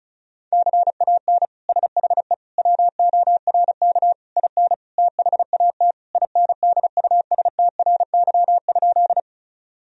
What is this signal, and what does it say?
Morse code, heard as a keyed tone, about 36 wpm, 700 Hz. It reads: CAN SHE WORK IN THAT INDUSTRY?
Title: Sentences from Top 1000 Words - Review Part 4 - 35wpm